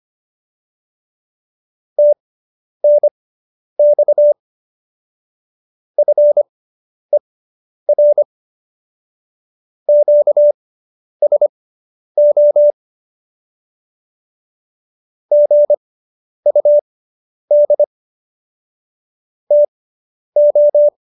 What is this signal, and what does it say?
Morse code, heard as a keyed tone, about 25 wpm, spaced as for 10 wpm, 590 Hz. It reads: TNX FER QSO GUD TO